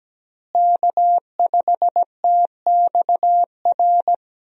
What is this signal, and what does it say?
Morse code, heard as a keyed tone, about 17 wpm, 700 Hz. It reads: K5TXR
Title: USA Call Signs - 5 Characters - Repeated 2x - 17wpm